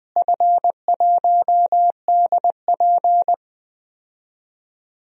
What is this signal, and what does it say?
Morse code, heard as a keyed tone, about 21 wpm, 700 Hz. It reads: F1DP
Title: DXCC-100 Call Signs - 4 Characters - Repeated 3x - 20wpm